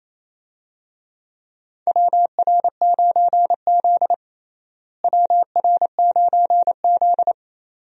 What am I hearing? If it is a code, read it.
Morse code, heard as a keyed tone, about 27 wpm, 700 Hz. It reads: WR9Z WR9Z
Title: USA Call Signs - 4 Characters - Repeated 2x - 28wpm